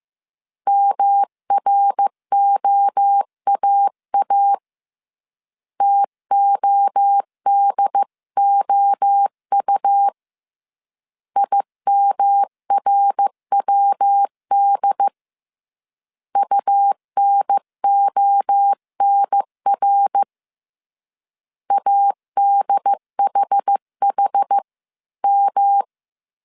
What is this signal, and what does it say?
Morse code, heard as a keyed tone, about 15 wpm, 800 Hz. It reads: MROAA TODOU IMRWD UNONR ADHHM
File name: CW / DET / 1 3 1500C S C 14wpm 2458